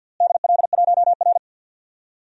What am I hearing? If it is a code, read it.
Morse code, heard as a keyed tone, about 46 wpm, 700 Hz. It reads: DL1R
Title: DXCC-100 Call Signs - 4 Characters - Repeated 3x - 50wpm